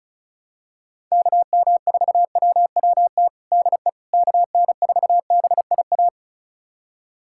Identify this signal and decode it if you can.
Morse code, heard as a keyed tone, about 34 wpm, 700 Hz. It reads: KM4WWT DE KN4BIA
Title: USA Callsign DE Callsign 35wpm